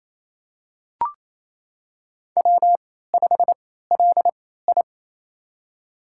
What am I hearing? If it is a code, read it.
Morse code, heard as a keyed tone, about 28 wpm, spaced as for 16 wpm, 700 Hz. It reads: W5LI
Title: USA Call Signs - 4 Characters - 3x Character Spacing - 28wpm